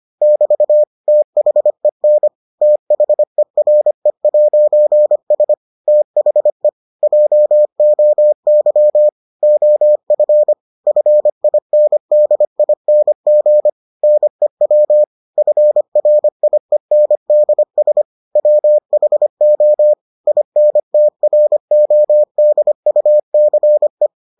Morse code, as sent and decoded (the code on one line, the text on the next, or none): -...- / - .... -. / - .... .-. .----. ... / - .... / .--- --- -.-- / --- ..-. / ..-. .. -. -.. .. -. --. / -. .-- / ..-. .-. .. -. -.. ... / .-- .... --- / .. -. - .-. --- -.. ..- -.-. .
<BT> THEN THERE'S THE JOY OF FINDING NEW FRIENDS WHO INTRODUCE